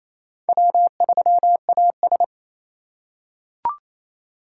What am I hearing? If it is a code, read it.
Morse code, heard as a keyed tone, about 28 wpm, 700 Hz. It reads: W3AS
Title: USA Call Signs - 4 Characters - 6x Character Spacing - 28wpm